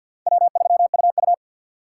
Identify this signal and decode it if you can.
Morse code, heard as a keyed tone, about 62 wpm, 700 Hz. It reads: W3UU
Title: Speed Racing - USA 1x2 Call Signs 50wpm